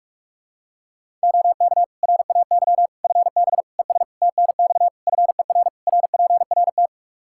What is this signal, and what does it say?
Morse code, heard as a keyed tone, about 46 wpm, 700 Hz. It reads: OK RAY FB ES TNX FER RPRT
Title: Practice QSO - Exchange 2 - Info - 45wpm